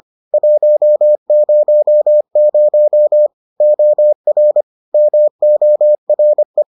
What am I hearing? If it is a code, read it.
Morse code, heard as a keyed tone, about 25 wpm, 600 Hz. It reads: 100 OR MORE